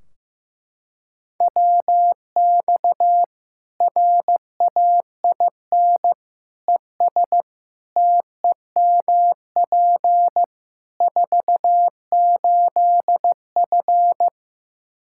Morse code, heard as a keyed tone, about 16 wpm, 700 Hz. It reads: WX RAIN ES TEMP 48F
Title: Practice QSO - Exchange 2 - Info - 15wpm